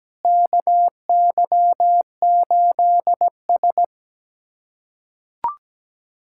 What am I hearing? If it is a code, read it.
Morse code, heard as a keyed tone, about 17 wpm, 700 Hz. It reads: KY8S
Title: USA Callsign DE Callsign - 4 Characters - 17wpm